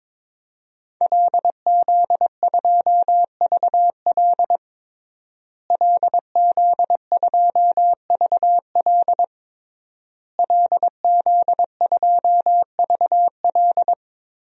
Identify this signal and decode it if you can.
Morse code, heard as a keyed tone, about 21 wpm, 700 Hz. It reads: LZ2VL LZ2VL LZ2VL
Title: DXCC-100 Call Signs - 5 Characters - Repeated 3x - 22wpm